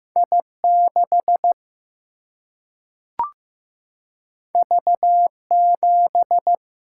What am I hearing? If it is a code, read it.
Morse code, heard as a keyed tone, about 16 wpm, 700 Hz. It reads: I6 V7
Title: Single-Letter Single-Number 15wpm